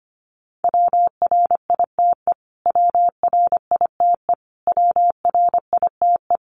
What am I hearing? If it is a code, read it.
Morse code, heard as a keyed tone, about 25 wpm, 700 Hz. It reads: WRITE WRITE WRITE
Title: Top 300 Words Repeated 25wpm